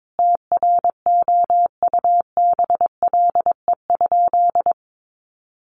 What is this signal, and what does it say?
Morse code, heard as a keyed tone, about 22 wpm, 700 Hz. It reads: TROUBLE?
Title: Sentences from Top 1200 Words 22wpm